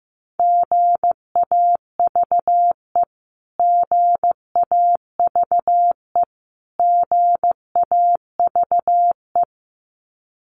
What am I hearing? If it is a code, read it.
Morse code, heard as a keyed tone, about 15 wpm, 700 Hz. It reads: GAVE GAVE GAVE
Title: Top 500 Words Repeated 15wpm